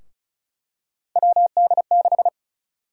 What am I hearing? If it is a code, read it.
Morse code, heard as a keyed tone, about 34 wpm, 700 Hz. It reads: WD6
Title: USA Call Sign Prefixes - 35wpm